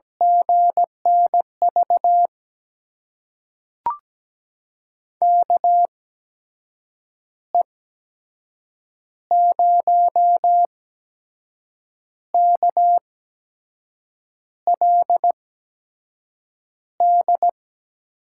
Morse code, heard as a keyed tone, about 17 wpm, spaced as for 6 wpm, 700 Hz. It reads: GNV KE0KLD
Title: USA Call Signs - 6 Characters - 8x Character Spacing - 17wpm